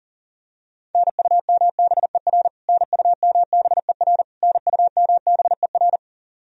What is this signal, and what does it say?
Morse code, heard as a keyed tone, about 40 wpm, 700 Hz. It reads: NUMBER NUMBER NUMBER
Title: Top 100 Words Repeated 40wpm